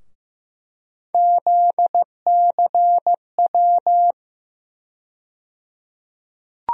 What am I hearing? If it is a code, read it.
Morse code, heard as a keyed tone, about 15 wpm, 700 Hz. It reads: ZCW
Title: USA Call Sign Postfixes - 3 Letters 15wpm